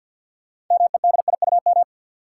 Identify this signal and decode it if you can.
Morse code, heard as a keyed tone, about 53 wpm, 700 Hz. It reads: MEDIUM
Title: Sentences from Top 900 Words - New Word Drill - 50wpm